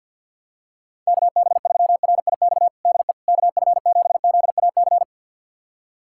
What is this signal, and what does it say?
Morse code, heard as a keyed tone, about 51 wpm, 700 Hz. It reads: KB3RIX DE KF7ZAC
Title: USA Callsign DE Callsign - 6 Characters - 50wpm